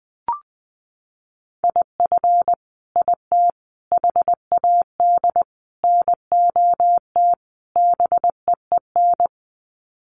Morse code, heard as a keyed tone, about 20 wpm, 700 Hz. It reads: IF IT HAD NOT BEEN
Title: Sets of 5 Words Encore 20wpm